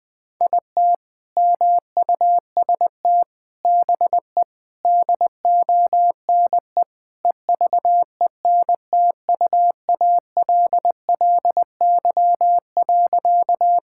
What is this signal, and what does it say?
Morse code, heard as a keyed tone, about 20 wpm, 700 Hz. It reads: IT MUST BE DONE EVENTUALLY.
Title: Sentences from Top 1000 Words - Review Part 6 - 20wpm